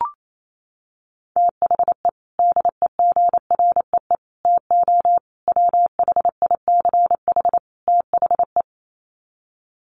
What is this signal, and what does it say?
Morse code, heard as a keyed tone, about 28 wpm, 700 Hz. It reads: THE DEGREE TO WHICH THE
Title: Sets of 5 Words Encore2 28wpm